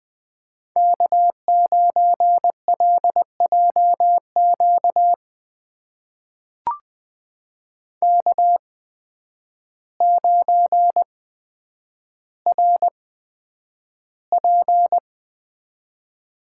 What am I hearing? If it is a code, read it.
Morse code, heard as a keyed tone, about 20 wpm, spaced as for 9 wpm, 700 Hz. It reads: K9LJQ K9RP
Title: USA Call Signs - 8x Character Spacing - 20wpm